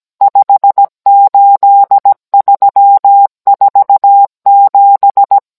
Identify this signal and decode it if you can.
Morse code, heard as a keyed tone, about 17 wpm, 800 Hz. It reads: 58347